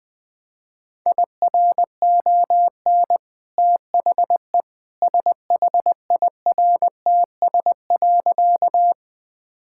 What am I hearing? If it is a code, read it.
Morse code, heard as a keyed tone, about 21 wpm, 700 Hz. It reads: IRON THE SHIRTS.
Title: Easy Sentences with Hints - 20wpm